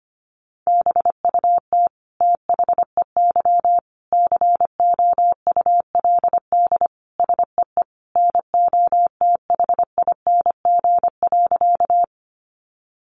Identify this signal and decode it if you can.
Morse code, heard as a keyed tone, about 25 wpm, 700 Hz. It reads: BUT THEY COULD SEE NOTHING.
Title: Sentences from Top 500 Words 25wpm